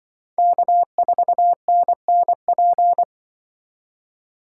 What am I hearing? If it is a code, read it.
Morse code, heard as a keyed tone, about 25 wpm, 700 Hz. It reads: K4NNP
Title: Speed Racing - USA 1x3 Call Signs 20wpm